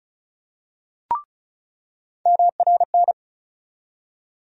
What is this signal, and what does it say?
Morse code, heard as a keyed tone, about 36 wpm, 700 Hz. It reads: MRN
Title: USA Call Sign Postfixes - 3 Letters 35wpm